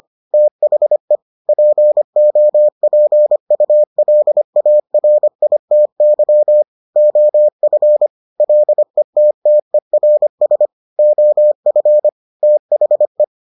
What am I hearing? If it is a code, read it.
Morse code, heard as a keyed tone, about 25 wpm, 600 Hz. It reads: THE POPULARITY OF LETTERS OF THE